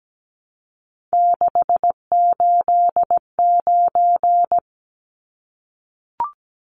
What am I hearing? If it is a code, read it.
Morse code, heard as a keyed tone, about 17 wpm, 700 Hz. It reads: 689
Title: Numbers - 3-Digit 17wpm